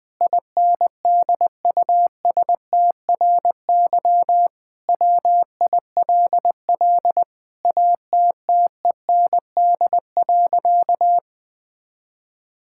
Morse code, heard as a keyed tone, about 20 wpm, 700 Hz. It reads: INDUSTRY WILL ATTEND.
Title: Sentences with 7-Word Phrases - 20wpm